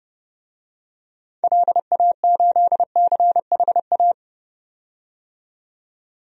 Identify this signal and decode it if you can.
Morse code, heard as a keyed tone, about 30 wpm, 700 Hz. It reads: LA8CHA
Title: DXCC-100 Call Signs - 6 Characters - 30wpm